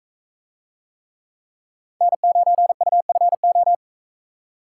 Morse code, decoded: N9UFO

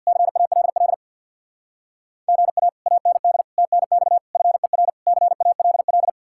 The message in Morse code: CALL GA AND TNX FER CALL

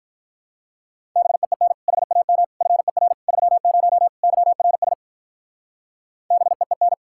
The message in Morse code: BEEN HAM FER 30 YRS BEEN

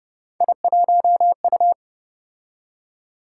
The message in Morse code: I1U

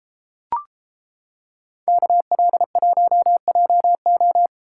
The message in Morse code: KL1JO

